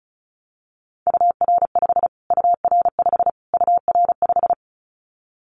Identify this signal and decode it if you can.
Morse code, heard as a keyed tone, about 34 wpm, 700 Hz. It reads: UR5 UR5 UR5
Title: Callsign Prefixes for Top 25 DXCC - Repeated 3x - 35wpm